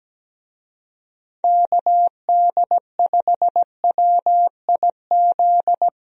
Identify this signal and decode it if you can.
Morse code, heard as a keyed tone, about 17 wpm, 700 Hz. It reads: KD5WIZ
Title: USA Call Signs - 6 Characters - Repeated 2x - 17wpm